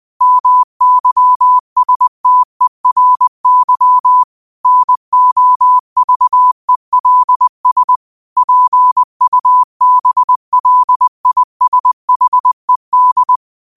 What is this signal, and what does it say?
Morse code, heard as a keyed tone, about 20 wpm, 1000 Hz. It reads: MYSTERY NOVELS PUBLISHED
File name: qotdfiles / 20wpm